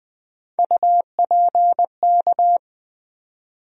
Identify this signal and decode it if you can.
Morse code, heard as a keyed tone, about 20 wpm, 700 Hz. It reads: UPK
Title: USA Call Sign Postfixes - 3 Letters 20wpm